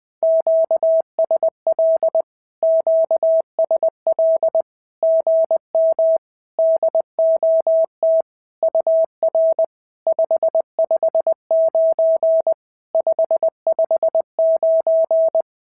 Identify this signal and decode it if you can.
Morse code, heard as a keyed tone, about 20 wpm, 650 Hz. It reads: QSL QSL GM DOT UR 559 559
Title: SOTA QSO Practice 20wpm